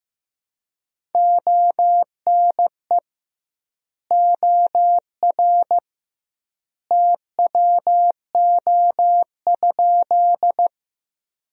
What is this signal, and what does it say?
Morse code, heard as a keyed tone, about 15 wpm, 700 Hz. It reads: ONE OR TWO?
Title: 2x Word-Spacing - Sentences from Top 100 Words - 15wpm